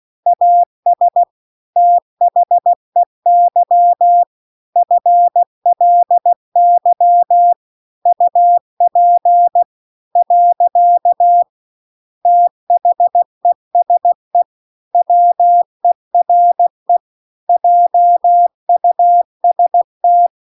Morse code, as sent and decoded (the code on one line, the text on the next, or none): .- ... / - .... -.-- / ..-. .-.. -.-- / ..- .--. / .-.-.- / - .... ... / .-- . .-. / .--- ..- ... -
AS THEY FLY UP . THESE WERE JUST